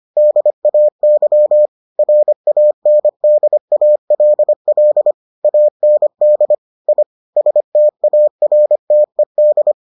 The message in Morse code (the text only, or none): DAY RANDALL AND I STARTED